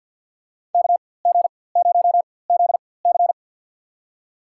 Morse code, heard as a keyed tone, about 52 wpm, spaced as for 24 wpm, 700 Hz. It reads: KG0ZC
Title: USA Call Signs - 5 Characters - 4x Character Spacing - 50wpm